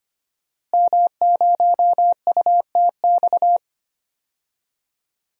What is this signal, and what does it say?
Morse code, heard as a keyed tone, about 25 wpm, 700 Hz. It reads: M0UTX
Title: DXCC-100 Call Signs - 5 Characters - Repeated 2x - 25wpm